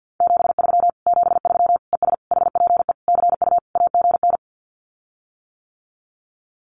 Morse code, heard as a keyed tone, about 50 wpm, 700 Hz. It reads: 73 73 ES HPE CU AGN
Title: Common QSO Phrases Encore 50wpm